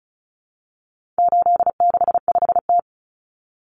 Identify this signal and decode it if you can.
Morse code, heard as a keyed tone, about 34 wpm, 700 Hz. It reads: 865T
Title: Common Cut Numbers - 4-Digit 35wpm